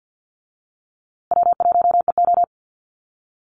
Morse code, heard as a keyed tone, about 50 wpm, 700 Hz. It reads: W1EO
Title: USA Call Signs Repeated 3x 50wpm